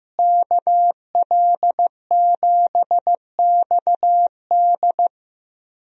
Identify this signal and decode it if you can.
Morse code, heard as a keyed tone, about 15 wpm, 700 Hz. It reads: KL7XD
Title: DXCC-100 Call Signs - 5 Characters - Repeated 3x - 15wpm